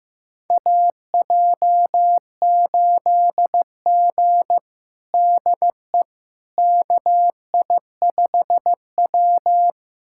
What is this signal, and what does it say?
Morse code, heard as a keyed tone, about 15 wpm, 700 Hz. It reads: AJ8G DE KI5W
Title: USA Callsign DE Callsign - 4 Characters - 15wpm